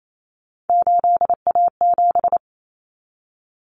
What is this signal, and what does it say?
Morse code, heard as a keyed tone, about 28 wpm, 700 Hz. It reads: 8A7
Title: Common Cut Numbers - 3-Digit 28wpm